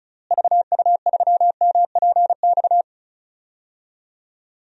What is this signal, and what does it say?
Morse code, heard as a keyed tone, about 34 wpm, 700 Hz. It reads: VU3MPX
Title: DXCC-100 Call Signs - 6 Characters - Repeated 2x - 35wpm